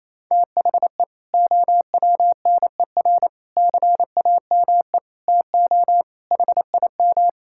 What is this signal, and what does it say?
Morse code, heard as a keyed tone, about 28 wpm, 700 Hz. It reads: THE OWNER CAME TO HIM
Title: Sentences from Top 1600 Words 28wpm